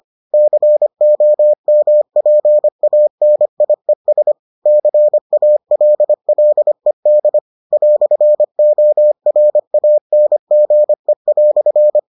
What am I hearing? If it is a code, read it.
Morse code, heard as a keyed tone, about 25 wpm, 600 Hz. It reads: COMPANIES CALLED "ORANGE"